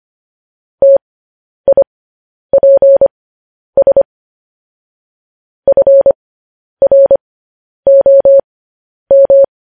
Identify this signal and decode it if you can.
Morse code, heard as a keyed tone, about 25 wpm, spaced as for 10 wpm, 550 Hz. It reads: TIPS FROM
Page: News Headlines in Morse Code at 10 WPM